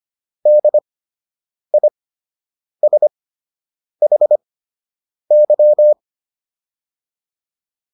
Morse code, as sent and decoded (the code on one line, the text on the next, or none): -.. .. ... .... -.--
DISHY